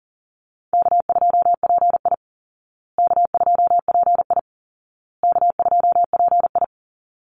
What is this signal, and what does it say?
Morse code, heard as a keyed tone, about 40 wpm, 700 Hz. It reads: K2PI K2PI K2PI
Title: Easy Call Signs Repeated 3x - 40wpm